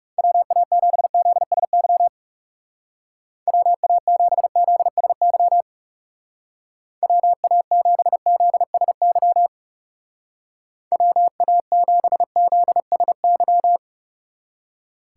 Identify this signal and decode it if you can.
Morse code, heard as a keyed tone, about 37 wpm, 700 Hz. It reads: WA7ZSY WA7ZSY WA7ZSY WA7ZSY
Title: Speed Racing - USA 2x3 Call Signs 30wpm